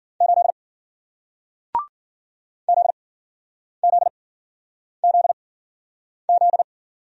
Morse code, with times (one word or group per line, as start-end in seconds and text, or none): 0.2-0.5 8
2.7-6.6 ZZZZ